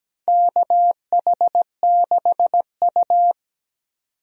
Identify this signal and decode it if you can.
Morse code, heard as a keyed tone, about 17 wpm, 700 Hz. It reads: KH6U
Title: DXCC-100 Call Signs - 4 Characters - 17wpm